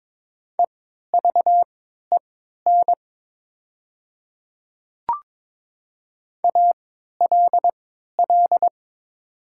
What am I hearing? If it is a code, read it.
Morse code, heard as a keyed tone, about 22 wpm, spaced as for 13 wpm, 700 Hz. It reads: EVEN ALL
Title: 3x Character Spacing - Top 200 Words - 22wpm